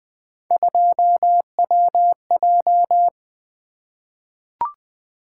Morse code, heard as a keyed tone, about 20 wpm, 700 Hz. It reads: 2WJ
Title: Three Letter-Number Combinations - 20wpm